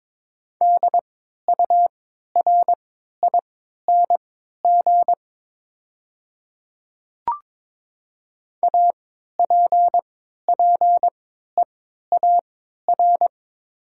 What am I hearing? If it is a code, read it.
Morse code, heard as a keyed tone, about 22 wpm, spaced as for 12 wpm, 700 Hz. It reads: DURING APPEAR
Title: 3x Character Spacing - Top 500 Words - 22wpm